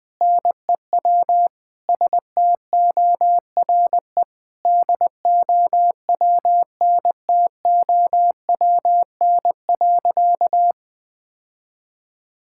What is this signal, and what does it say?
Morse code, heard as a keyed tone, about 21 wpm, 700 Hz. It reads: NEW STORE DOWNTOWN.
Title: Sentences with 6-Word Phrases - 20wpm